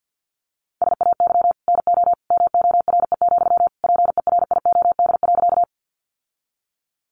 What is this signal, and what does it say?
Morse code, heard as a keyed tone, about 50 wpm, 700 Hz. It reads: SAY NO MORE, PERIOD.
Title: Sentences from Top 800 Words 50wpm